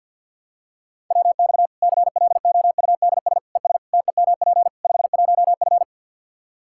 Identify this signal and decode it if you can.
Morse code, heard as a keyed tone, about 47 wpm, 700 Hz. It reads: WX CLOUDS ES TEMP 51F
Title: Practice QSO - Exchange 2 - Info - 50wpm